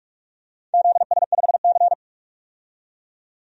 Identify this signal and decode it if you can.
Morse code, heard as a keyed tone, about 46 wpm, 700 Hz. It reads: ZS5C
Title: DXCC-100 Call Signs - 4 Characters - 45wpm